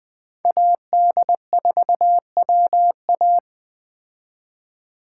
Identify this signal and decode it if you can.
Morse code, heard as a keyed tone, about 20 wpm, 700 Hz. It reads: AD4WA
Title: Speed Racing - USA 2x2 Call Signs 17wpm